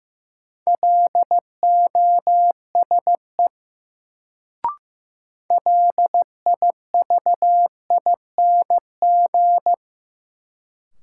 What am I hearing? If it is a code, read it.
Morse code, heard as a keyed tone, about 16 wpm, 700 Hz. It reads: LOSE LIVING
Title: Sentences from Top 1100 Words - New Word Drill Alt - 15wpm